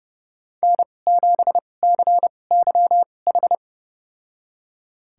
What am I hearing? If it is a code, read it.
Morse code, heard as a keyed tone, about 30 wpm, spaced as for 22 wpm, 700 Hz. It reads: N7CYH
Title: USA Call Signs - 5 Characters - 2x Character Spacing - 30wpm